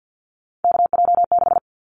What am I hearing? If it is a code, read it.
Morse code, heard as a keyed tone, about 50 wpm, 700 Hz. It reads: KJ6
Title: Callsign Prefixes for Top 25 DXCC - Repeated 3x - 50wpm